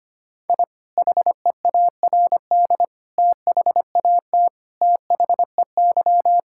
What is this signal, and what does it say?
Morse code, heard as a keyed tone, about 25 wpm, 700 Hz. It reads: I HEARD THAT THEY
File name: practice / 25wpm / Sentences with 2-Word Phrases - 25wpm